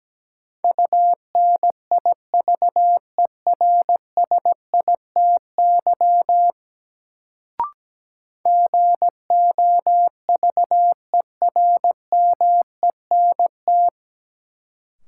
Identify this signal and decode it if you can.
Morse code, heard as a keyed tone, about 17 wpm, 700 Hz. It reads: UNIVERSITY GOVERMENT
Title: Sentences from Top 1300 Words - New Word Drill Alt - 17wpm